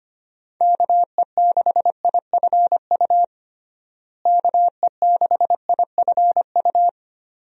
Morse code, heard as a keyed tone, about 25 wpm, 700 Hz. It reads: KE6IFU KE6IFU
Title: USA Call Signs - 6 Characters - Repeated 2x - 25wpm